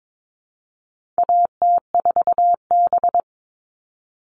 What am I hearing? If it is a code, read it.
Morse code, heard as a keyed tone, about 22 wpm, 700 Hz. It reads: AT4B
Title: All Cut Numbers - 4-Digit 22wpm